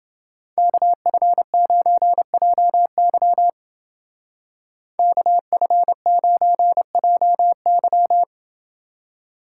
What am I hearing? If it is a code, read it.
Morse code, heard as a keyed tone, about 29 wpm, 700 Hz. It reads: KF9JY KF9JY